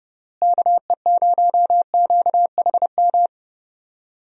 Morse code, heard as a keyed tone, about 32 wpm, 700 Hz. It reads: KE0QHM
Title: USA Call Signs - 6 Characters - Repeated 2x - 30wpm